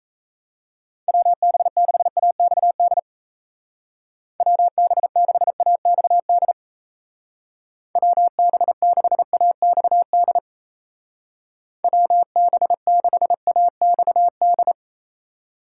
Morse code, heard as a keyed tone, about 35 wpm, 700 Hz. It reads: WB6AXD WB6AXD WB6AXD WB6AXD
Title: Speed Racing - USA Call Signs 28wpm